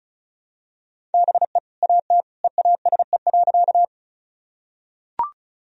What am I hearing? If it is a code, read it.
Morse code, heard as a keyed tone, about 34 wpm, 700 Hz. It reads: BE AT EASE.
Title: Sentences from Top 1000 Words - Review Part 1 - 35wpm